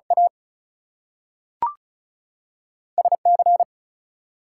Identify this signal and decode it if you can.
Morse code, heard as a keyed tone, about 36 wpm, spaced as for 6 wpm, 700 Hz. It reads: ASC
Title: US States and Canadian Provinces 35wpm